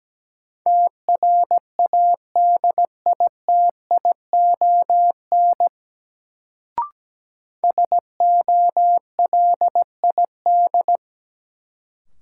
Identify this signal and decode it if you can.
Morse code, heard as a keyed tone, about 17 wpm, 700 Hz. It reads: TRADITION SOLID
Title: Sentences from Top 1500 Words - New Word Drill Alt - 17wpm